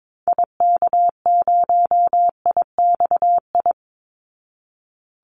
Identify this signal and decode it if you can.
Morse code, heard as a keyed tone, about 22 wpm, 700 Hz. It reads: IK0IXI
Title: Call Signs Repeated 3x - 22wpm